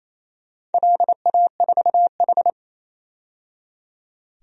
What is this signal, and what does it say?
Morse code, heard as a keyed tone, about 29 wpm, 700 Hz. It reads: LA4H
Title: DXCC-100 Call Signs - 4 Characters - 28wpm